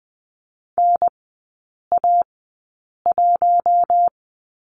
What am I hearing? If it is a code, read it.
Morse code, heard as a keyed tone, about 20 wpm, spaced as for 8 wpm, 700 Hz. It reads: NA1
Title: 041 Lesson 8e - Number 1 - Three Characters 20wpm